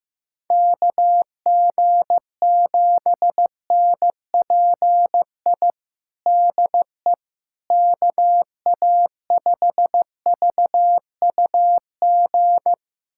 Morse code, heard as a keyed tone, about 15 wpm, 700 Hz. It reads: KG7NPI DE KA5VUG
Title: USA Callsign DE Callsign - 6 Characters - 15wpm